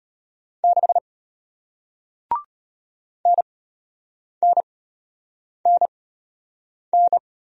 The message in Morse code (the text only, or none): 6 NNNN